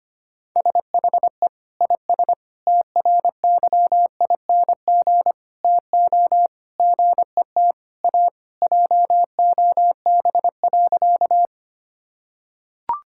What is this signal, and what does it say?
Morse code, heard as a keyed tone, about 25 wpm, 700 Hz. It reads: SHE IS TRYING TO GET A JOB.
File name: practice / 25wpm / Sentences with 3-Word Phrases - 25wpm